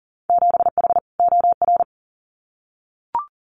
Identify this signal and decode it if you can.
Morse code, heard as a keyed tone, about 40 wpm, 700 Hz. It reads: 7H OR
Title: Winter Field Day Exchanges Encore 40wpm